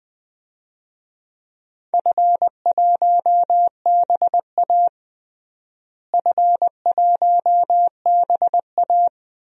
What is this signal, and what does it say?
Morse code, heard as a keyed tone, about 20 wpm, 700 Hz. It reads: F1BA F1BA
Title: DXCC-100 Call Signs - 4 Characters - Repeated 2x - 20wpm